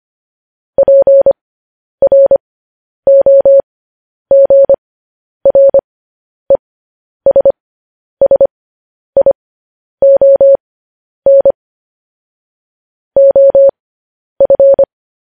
Morse code, heard as a keyed tone, about 25 wpm, spaced as for 10 wpm, 550 Hz. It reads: PROGRESSION OF
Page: news headlines in morse code at 10 wpm